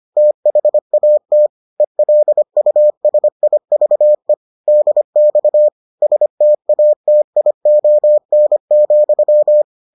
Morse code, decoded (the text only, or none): THAT ELUSIVE DX STATION,